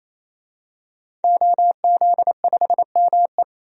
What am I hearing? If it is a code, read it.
Morse code, heard as a keyed tone, about 28 wpm, 700 Hz. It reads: OZ5ME
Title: DXCC-100 Call Signs - 5 Characters - 28wpm